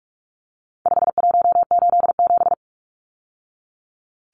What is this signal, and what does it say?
Morse code, heard as a keyed tone, about 44 wpm, 700 Hz. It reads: 5187
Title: Numbers - 4-Digit 45wpm